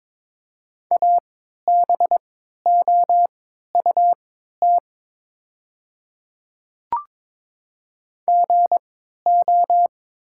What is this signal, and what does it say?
Morse code, heard as a keyed tone, about 23 wpm, spaced as for 13 wpm, 690 Hz. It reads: ABOUT GO
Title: 3x Character Spacing - Top 100 Words - 22wpm